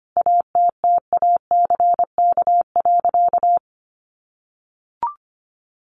Morse code, heard as a keyed tone, about 25 wpm, 700 Hz. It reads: ATTACK.
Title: Sentences from Top 1500 Words 25wpm